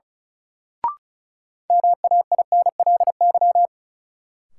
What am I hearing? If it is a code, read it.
Morse code, heard as a keyed tone, about 36 wpm, 700 Hz. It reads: MAINLY
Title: Sentences from Top 1400 Words - New Word Drill - 35wpm